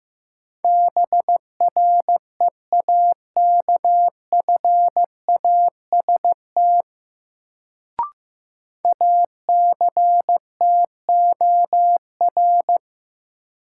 Morse code, heard as a keyed tone, about 16 wpm, 700 Hz. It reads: BREAKFAST ACTOR
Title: Sentences from Top 1500 Words - New Word Drill Alt - 15wpm